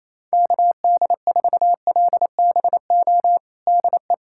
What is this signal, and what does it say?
Morse code, heard as a keyed tone, about 28 wpm, 700 Hz. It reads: KD4LBO DE
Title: USA Callsign DE Callsign - 6 Characters - 28wpm